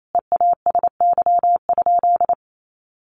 Morse code, heard as a keyed tone, about 28 wpm, 700 Hz. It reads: EASY?
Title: Sentences from Top 1500 Words 28wpm